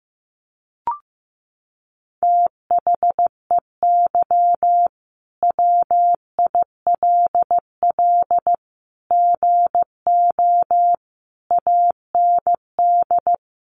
Morse code, heard as a keyed tone, about 15 wpm, 700 Hz. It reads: THEY WILL GO AND
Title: Sentences from Top 1100 Words 15wpm